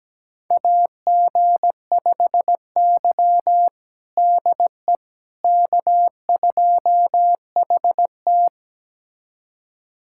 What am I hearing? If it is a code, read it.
Morse code, heard as a keyed tone, about 17 wpm, 700 Hz. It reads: AG5Y DE K2HT
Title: USA Callsign DE Callsign - 4 Characters - 17wpm